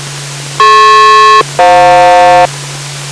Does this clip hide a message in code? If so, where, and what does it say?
Yes: aircraft SELCAL from 0.6 to 2.5 s, heard as two pairs of tones. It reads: DP-FJ